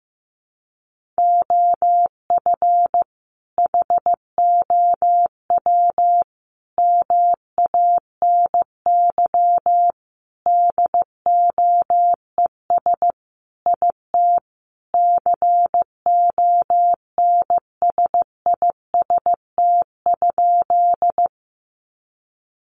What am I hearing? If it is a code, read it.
Morse code, heard as a keyed tone, about 15 wpm, 700 Hz. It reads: OF HOW MANY DOES IT CONSIST?
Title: Sentences from Top 1400 Words 15wpm